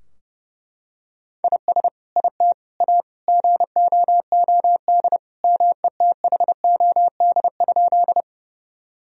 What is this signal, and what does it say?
Morse code, heard as a keyed tone, about 29 wpm, 700 Hz. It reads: IS IT A GOOD METHOD?
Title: Sentences from Top 1000 Words - Review Part 3 - 30wpm